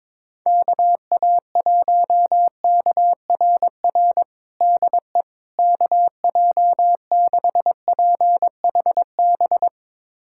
Morse code, heard as a keyed tone, about 22 wpm, 700 Hz. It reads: KA1KRR DE KJ6PHB
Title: USA Callsign DE Callsign - 6 Characters - 22wpm